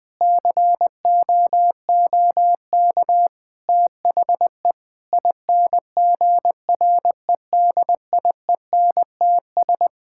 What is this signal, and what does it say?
Morse code, heard as a keyed tone, about 20 wpm, 700 Hz. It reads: COOK THE INGREDIENTS
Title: Sentences with 5-Word Phrases - Spoken Beginning - 20wpm